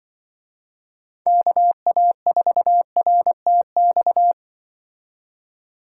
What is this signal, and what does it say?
Morse code, heard as a keyed tone, about 24 wpm, 700 Hz. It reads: KA4RTX
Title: Speed Racing - USA 2x3 Call Signs 20wpm